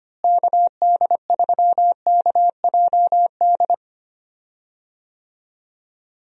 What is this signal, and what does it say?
Morse code, heard as a keyed tone, about 25 wpm, 700 Hz. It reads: KD3KJD
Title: USA Call Signs - 6 Characters - Repeated 2x - 25wpm